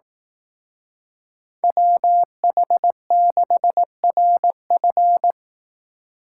Morse code, heard as a keyed tone, about 18 wpm, 700 Hz. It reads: WH6RF
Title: Speed Racing - USA 2x2 Call Signs 15wpm